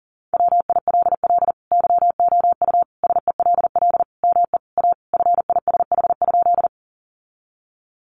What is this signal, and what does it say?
Morse code, heard as a keyed tone, about 40 wpm, 700 Hz. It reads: WILL YOU SELL ME A FISH?
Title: Sentences from Top 1100 Words 40wpm